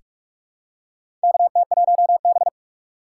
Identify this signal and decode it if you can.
Morse code, heard as a keyed tone, about 43 wpm, 700 Hz. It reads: KT1B
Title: K1USN SST QSO Practice - 45wpm